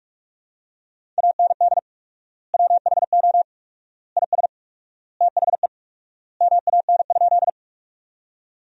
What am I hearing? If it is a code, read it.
Morse code, heard as a keyed tone, about 46 wpm, 700 Hz. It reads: AND WHO IS THE MAN?
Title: 4x Word-Spacing - Sentences from Top 200 Words - 45wpm